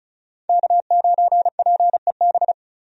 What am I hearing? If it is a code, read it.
Morse code, heard as a keyed tone, about 36 wpm, 700 Hz. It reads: K9PEB
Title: USA Call Signs - 5 Characters - 35wpm